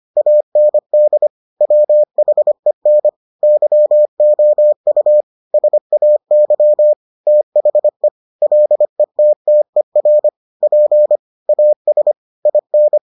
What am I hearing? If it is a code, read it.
Morse code, heard as a keyed tone, about 25 wpm, 600 Hz. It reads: AND WHEN YOU SAY THE LETTER P AS IN